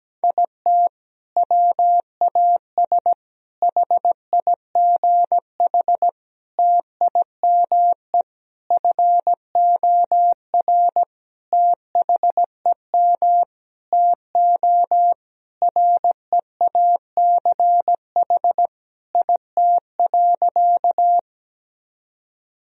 Morse code, heard as a keyed tone, about 17 wpm, 700 Hz. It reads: IT WAS HIGH TIME FOR THEM TO REACH IT.
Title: Sentences from Top 1000 Words - Review Part 2 - 17wpm